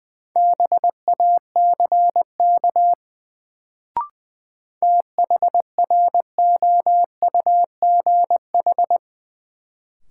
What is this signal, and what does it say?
Morse code, heard as a keyed tone, about 20 wpm, 700 Hz. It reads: BACK THROUGH
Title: Sets of 2 Words - New Word Drill - 20wpm